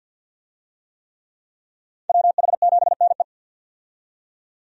Morse code, heard as a keyed tone, about 52 wpm, 700 Hz. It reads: WH7NE
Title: USA Call Signs - 5 Characters - 50wpm